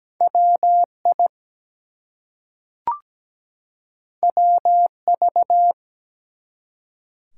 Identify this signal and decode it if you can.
Morse code, heard as a keyed tone, about 17 wpm, 700 Hz. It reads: WI WV
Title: US States and Canadian Provinces 17wpm